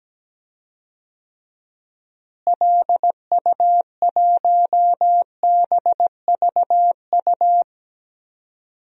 Morse code, heard as a keyed tone, about 17 wpm, 700 Hz. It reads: LU1BVU
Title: DXCC-100 Call Signs - 6 Characters - Repeated 3x - 17wpm